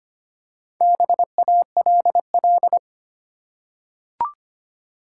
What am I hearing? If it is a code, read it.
Morse code, heard as a keyed tone, about 25 wpm, 690 Hz. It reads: BALL
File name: practice / 25wpm / Easy Words with Hints - 25wpm